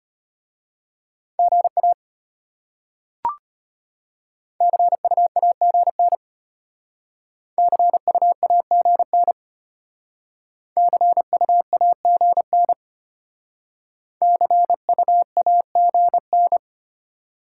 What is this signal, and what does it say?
Morse code, heard as a keyed tone, about 32 wpm, 700 Hz. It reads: GA CU AGN CU AGN CU AGN CU AGN